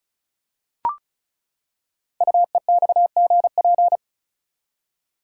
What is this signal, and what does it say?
Morse code, heard as a keyed tone, about 36 wpm, 700 Hz. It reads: UEXGP
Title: Code Groups - 5 Letters - 35wpm